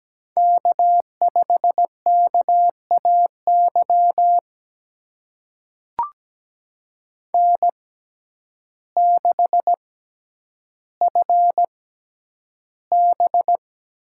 Morse code, heard as a keyed tone, about 17 wpm, spaced as for 9 wpm, 700 Hz. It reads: K5KAY N6FB
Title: USA Call Signs - 5 Characters - 6x Character Spacing - 17wpm